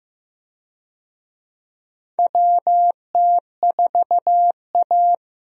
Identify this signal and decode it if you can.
Morse code, heard as a keyed tone, about 15 wpm, 700 Hz. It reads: WT4A
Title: USA Call Signs - 4 Characters - Repeated 3x - 15wpm